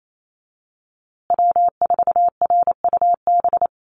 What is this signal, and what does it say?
Morse code, heard as a keyed tone, about 28 wpm, 700 Hz. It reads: W4RUB
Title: USA Call Signs Repeated 2x 28wpm